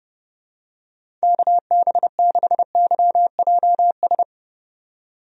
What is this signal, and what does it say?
Morse code, heard as a keyed tone, about 30 wpm, 700 Hz. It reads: KB6YJS
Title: USA Call Signs - 6 Characters - 30wpm